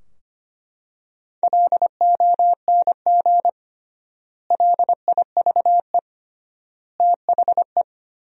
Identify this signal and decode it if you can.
Morse code, heard as a keyed tone, about 25 wpm, 700 Hz. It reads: LONG LIVE THE
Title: 3x Word-Spacing - Sentences from Top 400 Words - 25wpm